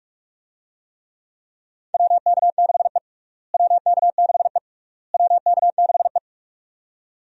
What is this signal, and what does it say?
Morse code, heard as a keyed tone, about 42 wpm, 700 Hz. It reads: WK6E WK6E WK6E
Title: USA Call Signs - 4 Characters - Repeated 3x - 45wpm